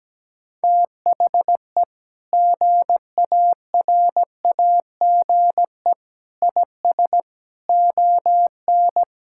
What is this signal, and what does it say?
Morse code, heard as a keyed tone, about 17 wpm, 700 Hz. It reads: THE GARAGE IS ON
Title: Sentences from Top 1600 Words 17wpm